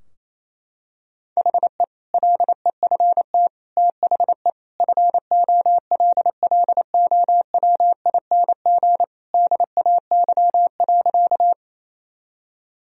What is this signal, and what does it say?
Morse code, heard as a keyed tone, about 28 wpm, 700 Hz. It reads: HE LEFT THE FOLLOWING DAY.
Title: Sentences from Top 1000 Words - Review Part 4 - 28wpm